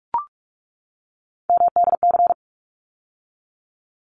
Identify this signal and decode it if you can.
Morse code, heard as a keyed tone, about 44 wpm, 700 Hz. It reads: MDC
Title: ARRL Sections 45wpm